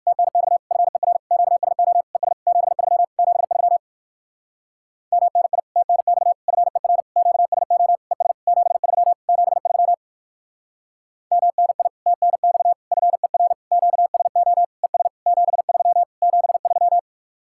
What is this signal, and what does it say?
Morse code, heard as a keyed tone, about 59 wpm, 700 Hz. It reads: TNX FER QSO ES 73 73 MNI TNX FER QSO ES 73 73 MNI TNX FER QSO ES 73 73